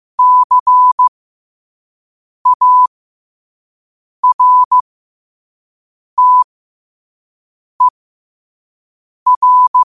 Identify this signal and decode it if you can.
Morse code, heard as a keyed tone, about 15 wpm, spaced as for 5 wpm, 1000 Hz. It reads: CARTER
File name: qotdfiles / 5wpm